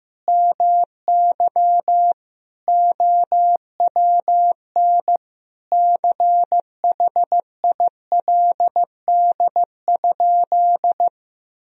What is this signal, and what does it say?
Morse code, heard as a keyed tone, about 16 wpm, 700 Hz. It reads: MY OWN CHILD?
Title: Sentences from Top 1000 Words - Review Part 2 - 15wpm